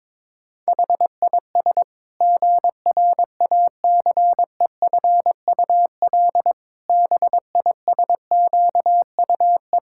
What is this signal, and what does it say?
Morse code, heard as a keyed tone, about 22 wpm, 700 Hz. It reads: HIS GRACEFUL BISQUE
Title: Pangram Challenge v2 - 22wpm